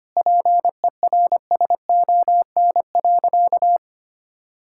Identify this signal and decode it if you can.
Morse code, heard as a keyed tone, about 25 wpm, 700 Hz. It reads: PERSON.
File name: practice / 25wpm / Sentences with 3-Word Phrases - 25wpm